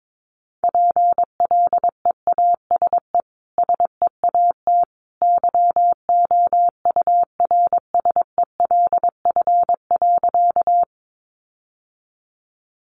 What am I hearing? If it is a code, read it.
Morse code, heard as a keyed tone, about 22 wpm, 700 Hz. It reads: PLEASE SEAT YOURSELF.
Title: Sentences from Top 1500 Words 22wpm